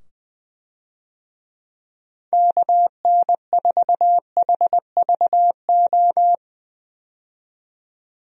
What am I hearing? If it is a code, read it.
Morse code, heard as a keyed tone, about 20 wpm, 700 Hz. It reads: KN4HVO
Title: USA Call Signs - 6 Characters - 20wpm